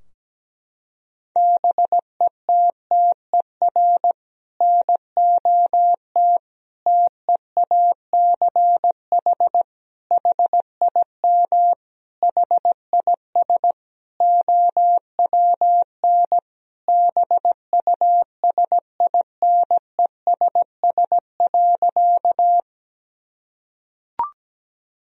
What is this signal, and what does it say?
Morse code, heard as a keyed tone, about 17 wpm, 700 Hz. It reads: BETTER NOT TEACH HIM HIS OWN BUSINESS.
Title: Sentences from Top 1000 Words - Review Part 2 - 17wpm